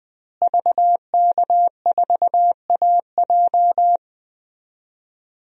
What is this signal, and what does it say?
Morse code, heard as a keyed tone, about 21 wpm, 700 Hz. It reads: VK4AJ
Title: DXCC-100 Call Signs - 5 Characters - Repeated 2x - 20wpm